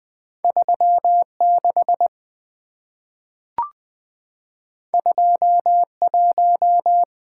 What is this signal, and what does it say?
Morse code, heard as a keyed tone, about 20 wpm, 700 Hz. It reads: 3 6 2 1
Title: Two Letter-Number Combinations - 20wpm